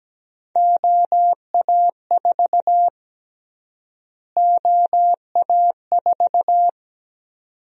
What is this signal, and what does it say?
Morse code, heard as a keyed tone, about 17 wpm, 700 Hz. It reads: OA4 OA4
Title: USA Call Sign Prefixes - Repeated 2x - 17wpm